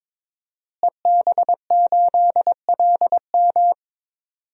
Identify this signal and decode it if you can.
Morse code, heard as a keyed tone, about 22 wpm, 700 Hz. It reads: EB8LM
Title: DXCC-100 Call Signs - 5 Characters - Repeated 3x - 22wpm